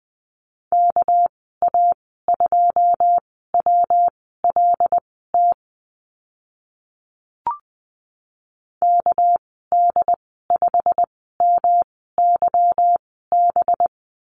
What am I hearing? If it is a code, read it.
Morse code, heard as a keyed tone, about 20 wpm, spaced as for 14 wpm, 700 Hz. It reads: KA2WLT KD5MYB
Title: 231 Lesson 37g - Letter x - Call Signs 20wpm